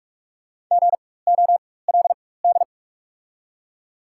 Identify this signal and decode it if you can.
Morse code, heard as a keyed tone, about 44 wpm, spaced as for 21 wpm, 690 Hz. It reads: GOLD
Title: 4x Character Spacing - Top 500 Words - 45wpm